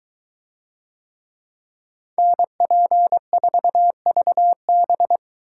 Morse code, heard as a keyed tone, about 21 wpm, 700 Hz. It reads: NP4VB